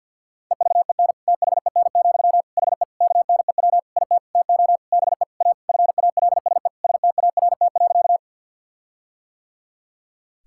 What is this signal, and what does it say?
Morse code, heard as a keyed tone, about 50 wpm, 700 Hz. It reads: EVEN THEN, HE KNEW IT TO BE A FALSE START.
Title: Sentences from Top 1600 Words 50wpm